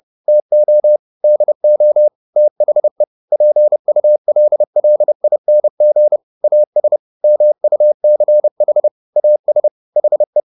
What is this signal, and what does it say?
Morse code, heard as a keyed tone, about 30 wpm, 600 Hz. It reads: TO DO THE PULLING AS MUCH AS HE